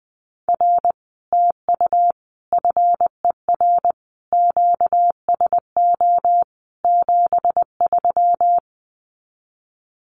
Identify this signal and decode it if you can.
Morse code, heard as a keyed tone, about 20 wpm, 700 Hz. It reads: R TU FER QSO 73
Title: Common QSO Phrases Encore 20wpm